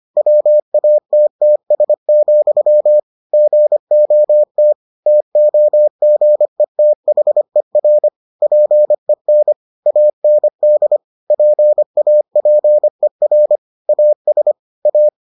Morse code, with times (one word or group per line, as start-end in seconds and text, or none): 0.2-3.0 WATTS,
3.3-4.7 GOT
5.1-8.1 TOGETHER
8.4-9.5 PEN
9.9-11.0 AND
11.3-13.6 PAPER
13.9-14.5 AS
14.9-15.1 A